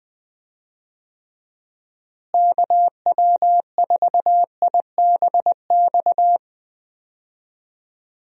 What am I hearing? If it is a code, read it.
Morse code, heard as a keyed tone, about 20 wpm, 700 Hz. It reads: KW4IBX
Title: USA Call Signs - 6 Characters - Repeated 3x - 20wpm